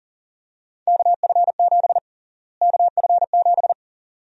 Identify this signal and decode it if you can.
Morse code, heard as a keyed tone, about 40 wpm, 700 Hz. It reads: KF7 KF7
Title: USA Call Sign Prefixes - Repeated 2x - 40wpm